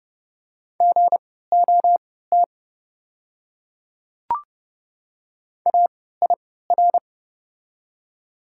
Morse code, heard as a keyed tone, about 29 wpm, spaced as for 17 wpm, 690 Hz. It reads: GOT AIR
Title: 3x Character Spacing - Top 300 Words - 30wpm